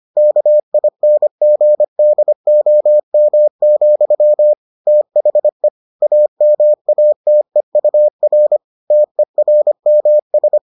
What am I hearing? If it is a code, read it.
Morse code, heard as a keyed tone, about 25 wpm, 600 Hz. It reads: KINGDOM, THE AMATEUR TERMS